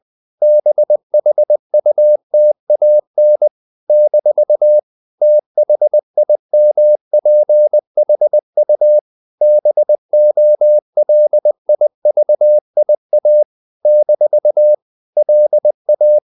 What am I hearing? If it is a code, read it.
Morse code, heard as a keyed tone, about 20 wpm, 600 Hz. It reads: BHUTAN - THIMPHU BOLIVIA - LA
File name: CW / Practice / NationalCapitals20